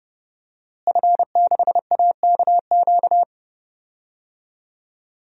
Morse code, heard as a keyed tone, about 29 wpm, 700 Hz. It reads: F6AKQ
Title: DXCC-100 Call Signs - 5 Characters - Repeated 3x - 30wpm